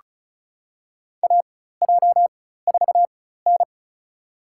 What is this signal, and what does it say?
Morse code, heard as a keyed tone, about 34 wpm, spaced as for 16 wpm, 700 Hz. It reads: AJ4N
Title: USA Call Signs - 4x Character Spacing - 35wpm